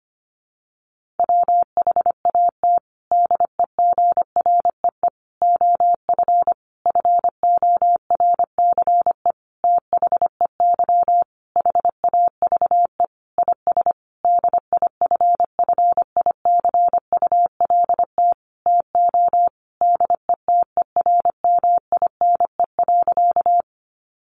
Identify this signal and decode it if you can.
Morse code, heard as a keyed tone, about 25 wpm, 700 Hz. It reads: WHAT DEGREE OF FORCE THEY HAVE IS DIFFICULT TO DETERMINE.
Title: Sentences from Top 1500 Words 25wpm